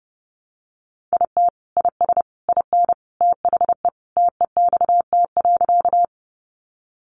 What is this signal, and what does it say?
Morse code, heard as a keyed tone, about 30 wpm, 700 Hz. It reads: IT IS IN THE TEXT.
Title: Sentences from Top 1200 Words 30wpm